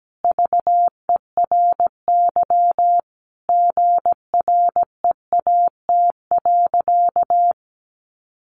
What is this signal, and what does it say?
Morse code, heard as a keyed tone, about 17 wpm, 700 Hz. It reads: VERY GREAT.
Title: Sentences from Top 1200 Words 17wpm